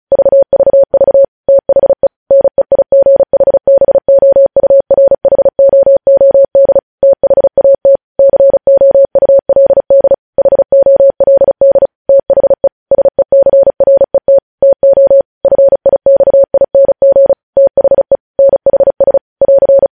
Morse code, heard as a keyed tone, about 36 wpm, 550 Hz. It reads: VVV THE NEIGHBOURHOOD THAT COULD HOLD THE SECRET TO FIXING THE NHS <AR>
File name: podcast / News headlines-1x-35x35